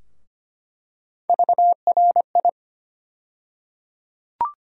Morse code, heard as a keyed tone, about 25 wpm, 700 Hz. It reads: VRI
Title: USA Call Sign Postfixes - 3 Letters 25wpm